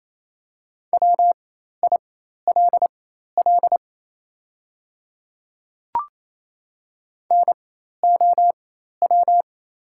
Morse code, heard as a keyed tone, about 28 wpm, spaced as for 13 wpm, 700 Hz. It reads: WILL NOW
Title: 4x Character Spacing - Top 100 Words - 28wpm